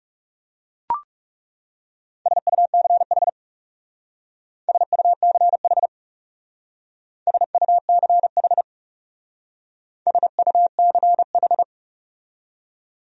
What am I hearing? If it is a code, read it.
Morse code, heard as a keyed tone, about 38 wpm, 700 Hz. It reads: SUCH SUCH SUCH SUCH